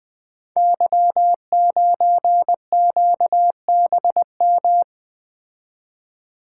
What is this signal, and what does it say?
Morse code, heard as a keyed tone, about 20 wpm, 700 Hz. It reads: Y9QBM
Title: Code Groups - 5 Letter:Numbers - 20wpm